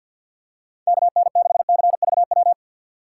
Morse code, heard as a keyed tone, about 51 wpm, 700 Hz. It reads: KN6CVW
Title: USA Call Signs - 3x Character Spacing - 50wpm